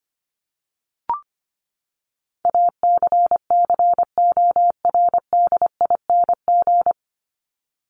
Morse code, heard as a keyed tone, about 25 wpm, 700 Hz. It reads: ACCORDING
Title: LY Suffix - Top 100 Words with Base Words 25wpm